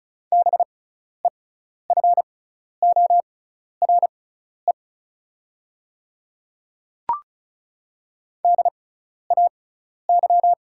Morse code, heard as a keyed tone, about 34 wpm, spaced as for 12 wpm, 690 Hz. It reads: BEFORE DAY
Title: 6x Character Spacing - Top 200 Words - 35wpm